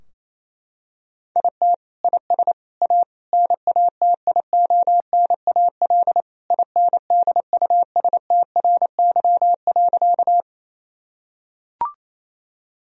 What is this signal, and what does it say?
Morse code, heard as a keyed tone, about 28 wpm, 700 Hz. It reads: IT IS A NATIONAL INDUSTRY.
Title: Sentences from Top 1000 Words - Review Part 4 - 28wpm